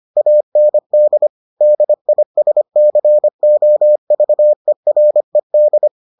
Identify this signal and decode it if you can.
Morse code, heard as a keyed tone, about 25 wpm, 600 Hz. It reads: AND DISCOVERED